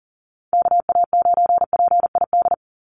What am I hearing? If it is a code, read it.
Morse code, heard as a keyed tone, about 40 wpm, 700 Hz. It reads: KA9PID
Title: USA Call Signs 40wpm